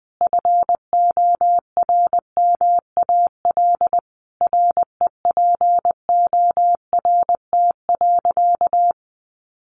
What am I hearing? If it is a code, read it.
Morse code, heard as a keyed tone, about 20 wpm, 700 Hz. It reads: FORMAL REPORT.
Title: Sentences from Top 1300 Words 20wpm